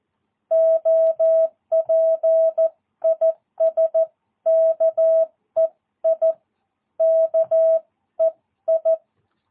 Morse code, heard as a keyed tone, about 14 wpm, 660 Hz. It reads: OP IS KEI KEI
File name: jh8cbh / 2022cw